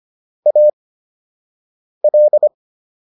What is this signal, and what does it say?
Morse code, heard as a keyed tone, about 25 wpm, spaced as for 6 wpm, 600 Hz. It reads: AL